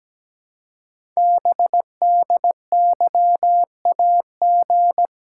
Morse code, heard as a keyed tone, about 17 wpm, 700 Hz. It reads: BDYAG
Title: Code Groups - 5 Letters - 17wpm